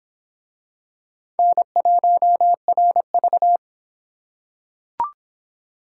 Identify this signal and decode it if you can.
Morse code, heard as a keyed tone, about 26 wpm, 700 Hz. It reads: N1RV